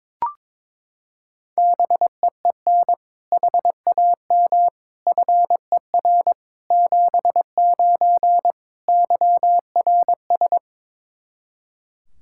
Morse code, heard as a keyed tone, about 22 wpm, 700 Hz. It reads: BEEN HAM FER 79 YRS
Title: Practice QSO - Exchange 3 - More Info - 22wpm